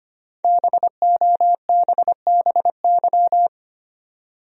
Morse code, heard as a keyed tone, about 25 wpm, 700 Hz. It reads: BOBBY
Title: USA Common Names - Max 5 Letters - 25wpm